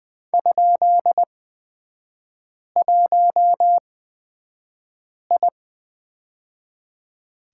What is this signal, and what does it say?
Morse code, heard as a keyed tone, about 20 wpm, spaced as for 5 wpm, 700 Hz. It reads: ?1I
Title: 243 Lesson 39e - Letter z - Three Characters 20wpm